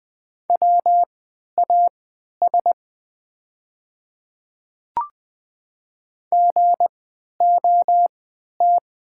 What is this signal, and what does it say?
Morse code, heard as a keyed tone, about 20 wpm, spaced as for 11 wpm, 690 Hz. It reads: WAS GOT
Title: 3x Character Spacing - Top 500 Words - 20wpm